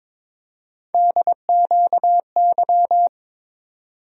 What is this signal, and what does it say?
Morse code, heard as a keyed tone, about 22 wpm, 700 Hz. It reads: DQY
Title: USA Call Sign Postfixes - 3 Letters 22wpm